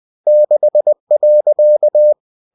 Morse code, heard as a keyed tone, about 20 wpm, 600 Hz. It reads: 6.
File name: cw-books / winnie-the-pooh / winnie-20wpm-page-0064